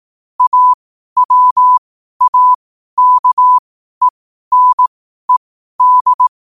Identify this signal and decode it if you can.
Morse code, heard as a keyed tone, about 18 wpm, spaced as for 12 wpm, 1000 Hz. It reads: AWAKENED